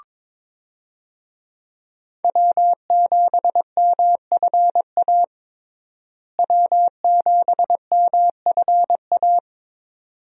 Morse code, heard as a keyed tone, about 22 wpm, 700 Hz. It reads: W7MFA W7MFA
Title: USA Call Signs - 5 Characters - Repeated 2x - 22wpm